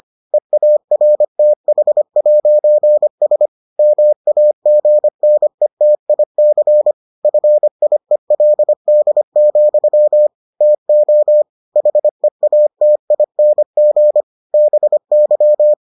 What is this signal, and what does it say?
Morse code, heard as a keyed tone, about 25 wpm, 600 Hz. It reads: EARTH'S MAGNETIC FIELD, TO HEATING BY